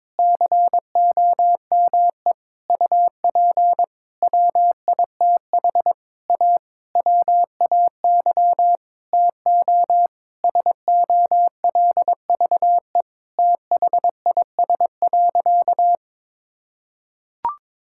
Morse code, heard as a keyed tone, about 22 wpm, 700 Hz. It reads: COME UP WITH A WAY TO SOLVE THIS.
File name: Sentences with 7-Word Phrases - 22wpm